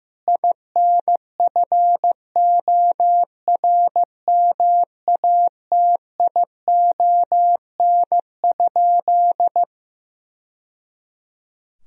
Morse code, heard as a keyed tone, about 15 wpm, 700 Hz. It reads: INFORMATION?
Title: Sentences from Top 1000 Words - Review Part 4 - 15wpm